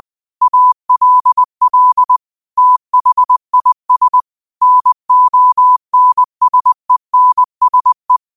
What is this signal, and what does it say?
Morse code, heard as a keyed tone, about 20 wpm, 1000 Hz. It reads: ALL THIS NONSENSE